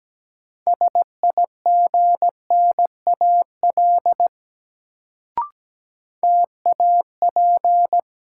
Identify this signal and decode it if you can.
Morse code, heard as a keyed tone, about 17 wpm, 700 Hz. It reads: SIGNAL TAP
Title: Sentences from Top 1400 Words - New Word Drill - 17wpm